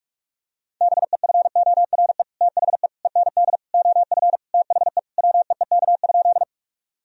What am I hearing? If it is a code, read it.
Morse code, heard as a keyed tone, about 46 wpm, 700 Hz. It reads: BEFORE THE END OF THE WEEK?
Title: Sentences from Top 1000 Words - Review Part 2 - 45wpm